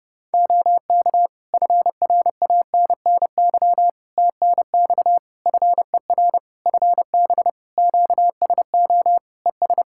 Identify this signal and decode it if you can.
Morse code, heard as a keyed tone, about 28 wpm, 700 Hz. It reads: OK FRANNY TNX FER FB QSO ES
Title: Practice QSO - Exchange 4 - Ending - 30wpm